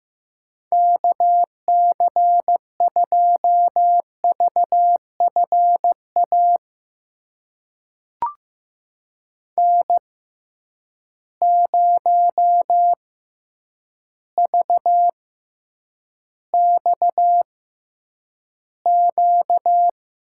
Morse code, heard as a keyed tone, about 15 wpm, spaced as for 8 wpm, 700 Hz. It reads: KC2VFA N0VXQ
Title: USA Call Signs - 6x Character Spacing - 15wpm